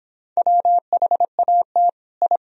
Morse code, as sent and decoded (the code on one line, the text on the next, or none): .-- .... .- - / ..
WHAT I